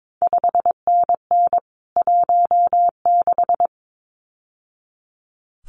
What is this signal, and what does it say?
Morse code, heard as a keyed tone, about 22 wpm, 700 Hz. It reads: 5NN 16
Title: CQ WW DX Contest Exchange - 22wpm